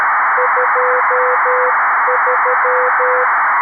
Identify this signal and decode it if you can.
Morse code, heard as a keyed tone, about 13 wpm, 500 Hz, with noise. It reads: 23
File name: arissat / recordings / ARISSat 11Sept2011 1354utc 60frames